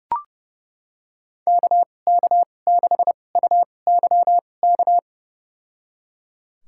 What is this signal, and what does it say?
Morse code, heard as a keyed tone, about 30 wpm, spaced as for 22 wpm, 700 Hz. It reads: KK6UYK
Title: USA Call Signs - 2x Character Spacing - 30wpm